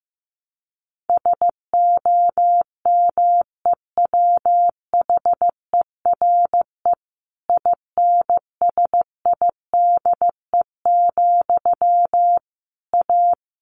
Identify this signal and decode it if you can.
Morse code, heard as a keyed tone, about 15 wpm, 700 Hz. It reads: SOMEWHERE INSIDE, A
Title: A Sound Of Thunder 15wpm